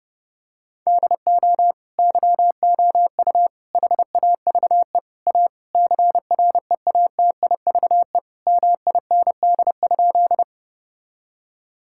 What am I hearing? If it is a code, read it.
Morse code, heard as a keyed tone, about 30 wpm, 700 Hz. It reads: DO YOU HAVE A CREATIVE MIND?
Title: Sentences from Top 1000 Words - Review Part 5 - 30wpm